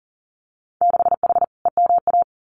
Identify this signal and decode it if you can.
Morse code, heard as a keyed tone, about 39 wpm, 700 Hz. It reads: 6H EMA
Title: Winter Field Day Exchanges Encore 40wpm